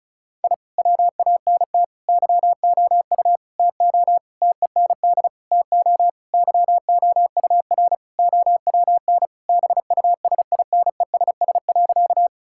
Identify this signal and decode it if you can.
Morse code, heard as a keyed tone, about 36 wpm, 700 Hz. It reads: I WANT YOU TO TEND TO YOUR OWN BUSINESS.
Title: Sentences from Top 1000 Words - Review Part 5 - 35wpm